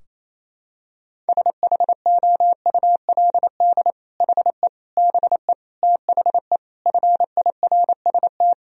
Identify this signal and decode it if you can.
Morse code, heard as a keyed tone, about 27 wpm, 700 Hz. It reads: SHOULD HE BE THE FIRST
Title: Sentences from Top 1000 Words - Review Part 4 - 28wpm